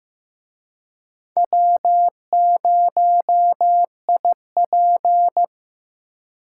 Morse code, heard as a keyed tone, about 18 wpm, 700 Hz. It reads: W0IP
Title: Speed Racing - USA 1x2 Call Signs 15wpm